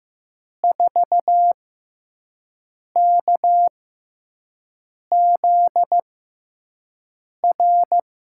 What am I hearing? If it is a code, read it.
Morse code, heard as a keyed tone, about 15 wpm, spaced as for 5 wpm, 700 Hz. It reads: 4KZR
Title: USA Call Signs - 6 Characters - 6x Character Spacing - 15wpm